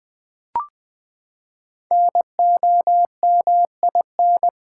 Morse code, heard as a keyed tone, about 20 wpm, 700 Hz. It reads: NOMIN
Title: Build A Word - 5 Letters At A Time - 10-Letter Words - 20wpm